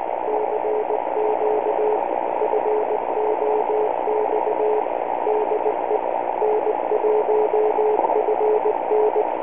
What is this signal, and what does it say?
Morse code, heard as a keyed tone, about 19 wpm, 430 Hz, with radio noise. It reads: CQ FOX DE N1FN